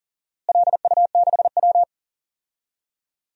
Morse code, heard as a keyed tone, about 40 wpm, 700 Hz. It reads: LU6W
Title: DXCC-100 Call Signs - 4 Characters - Repeated 3x - 40wpm